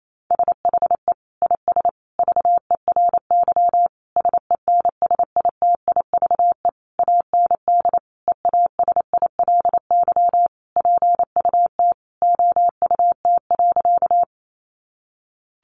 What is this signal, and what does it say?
Morse code, heard as a keyed tone, about 28 wpm, 700 Hz. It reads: SHE IS VERY SENSITIVE AND EASILY PUT OUT.
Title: Sentences from Top 1400 Words 28wpm